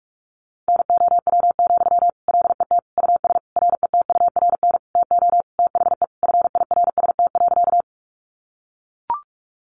Morse code, heard as a keyed tone, about 44 wpm, 700 Hz. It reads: NOW, LET US RETURN TO THE FIRST.
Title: Sentences from Top 800 Words 45wpm